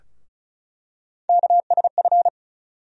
Morse code, heard as a keyed tone, about 36 wpm, 700 Hz. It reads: KSF
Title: USA Call Sign Postfixes - 3 Letters 35wpm